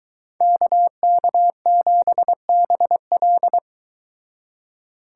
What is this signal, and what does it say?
Morse code, heard as a keyed tone, about 22 wpm, 700 Hz. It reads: KK7BL